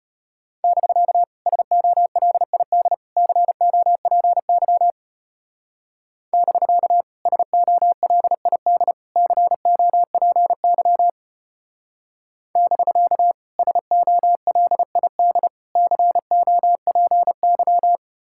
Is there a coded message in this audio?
Yes: Morse code from 0.6 to 18.0 s, heard as a keyed tone, about 32 wpm, 700 Hz. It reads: <BK> SOLID COPY <BK> SOLID COPY <BK> SOLID COPY